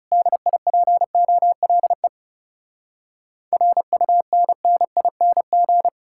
Morse code, heard as a keyed tone, about 40 wpm, 700 Hz. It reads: DIPOLE RUNNING